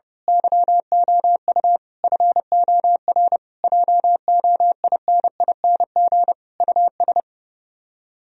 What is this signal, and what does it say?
Morse code, heard as a keyed tone, about 28 wpm, 700 Hz. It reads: YOU FOR JOINING US
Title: Speed-Racing - Sets of 5 Words 30wpm